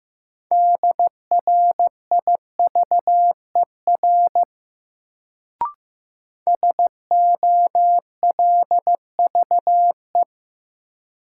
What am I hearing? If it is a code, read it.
Morse code, heard as a keyed tone, about 15 wpm, 700 Hz. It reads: DRIVER SOLVE
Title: Sentences from Top 1000 Words - New Word Drill - 15wpm